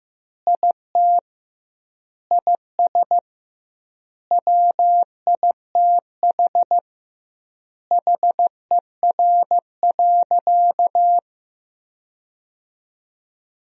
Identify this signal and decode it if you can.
Morse code, heard as a keyed tone, about 15 wpm, 700 Hz. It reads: IT IS WITH HER.
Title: 2x Word-Spacing - Sentences from Top 100 Words - 15wpm